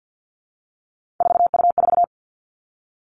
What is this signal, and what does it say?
Morse code, heard as a keyed tone, about 50 wpm, 700 Hz. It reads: 4U4
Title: Three Challenging Letter-Number Combinations 50wpm